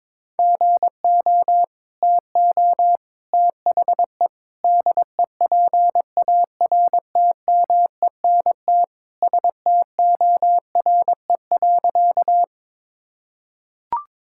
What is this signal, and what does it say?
Morse code, heard as a keyed tone, about 21 wpm, 700 Hz. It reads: GO TO THE DEPARTMENT STORE.
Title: Sentences from Top 1000 Words - Review Part 5 - 22wpm